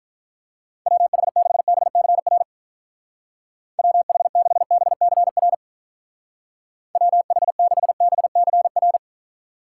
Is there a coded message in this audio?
Yes: Morse code from 0.9 to 9.0 s, heard as a keyed tone, about 45 wpm, 700 Hz. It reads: WH6BCR WH6BCR WH6BCR